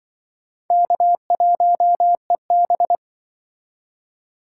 Morse code, heard as a keyed tone, about 26 wpm, 700 Hz. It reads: K1EB